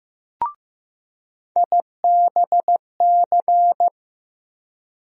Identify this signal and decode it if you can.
Morse code, heard as a keyed tone, about 15 wpm, 700 Hz. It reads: IBC
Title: USA Call Sign Postfixes - 3 Letters 15wpm